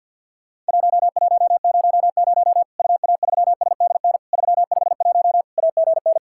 Morse code, heard as a keyed tone, about 50 wpm, 700 Hz. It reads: 1100 UA3SDN 351 AGN
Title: Worked All Europe DX Contest - QTC Exchange with Repeats - 50wpm